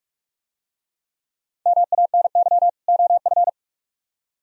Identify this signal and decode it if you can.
Morse code, heard as a keyed tone, about 46 wpm, 700 Hz. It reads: MANY OF